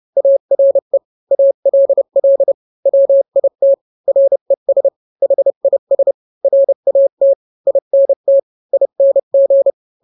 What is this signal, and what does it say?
Morse code, heard as a keyed tone, about 30 wpm, 550 Hz. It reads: ARE ALL WIT RES HIS RAT INT ING